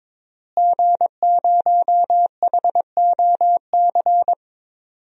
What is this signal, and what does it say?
Morse code, heard as a keyed tone, about 22 wpm, 700 Hz. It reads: G0HOC
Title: DXCC-100 Call Signs - 5 Characters - Repeated 3x - 22wpm